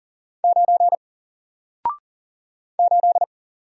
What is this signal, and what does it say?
Morse code, heard as a keyed tone, about 40 wpm, 700 Hz. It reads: 9 8
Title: Single Letter-Number Alt 40wpm